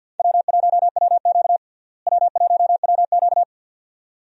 Morse code, heard as a keyed tone, about 49 wpm, 700 Hz. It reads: W1WQ W1WQ
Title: USA Call Signs - 4 Characters - Repeated 2x - 50wpm